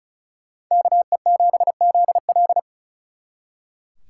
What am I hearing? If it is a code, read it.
Morse code, heard as a keyed tone, about 45 wpm, 700 Hz. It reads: KE7ZL